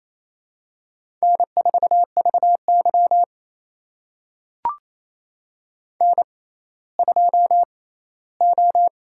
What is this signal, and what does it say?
Morse code, heard as a keyed tone, about 28 wpm, spaced as for 16 wpm, 700 Hz. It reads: N4VY N2O
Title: USA Call Signs - 6x Character Spacing - 28wpm